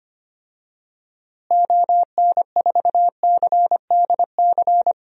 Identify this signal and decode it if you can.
Morse code, heard as a keyed tone, about 25 wpm, 700 Hz. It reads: ON4CDC
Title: DXCC-100 Call Signs - 6 Characters - Repeated 3x - 25wpm